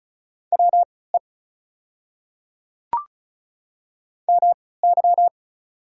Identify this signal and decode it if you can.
Morse code, heard as a keyed tone, about 34 wpm, spaced as for 20 wpm, 690 Hz. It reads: WE MY
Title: 3x Character Spacing - Top 100 Words - 35wpm